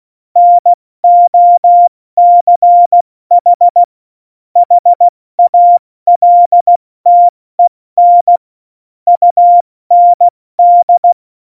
Morse code, heard as a keyed tone, about 16 wpm, 700 Hz. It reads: NOCH HALTEN UND